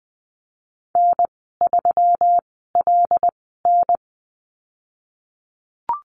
Morse code, heard as a keyed tone, about 19 wpm, spaced as for 14 wpm, 700 Hz. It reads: N3LN
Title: 128 Lesson 21g - Number 6 - Call Signs 20wpm